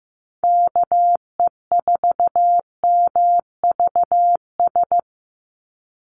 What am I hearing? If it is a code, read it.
Morse code, heard as a keyed tone, about 15 wpm, 700 Hz. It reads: KE4MVS
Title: USA Call Signs 15wpm